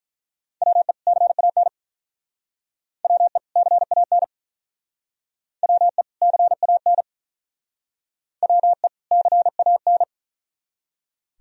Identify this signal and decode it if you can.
Morse code, heard as a keyed tone, about 44 wpm, 700 Hz. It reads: WE CAN WE CAN WE CAN WE CAN
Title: Speed-Racing - Sets of 2 Words 35wpm